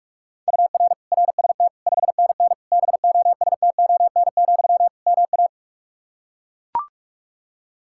Morse code, heard as a keyed tone, about 42 wpm, 700 Hz. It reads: UR RST 5NN BOSTON, MA
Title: Practice QSO - Exchange 1 - Beginning - 45wpm